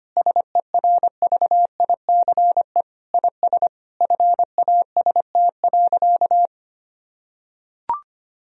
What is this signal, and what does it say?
Morse code, heard as a keyed tone, about 25 wpm, 700 Hz. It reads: SERVICE IS FAST.
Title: Sentences from Top 1000 Words - Review Part 4 - 25wpm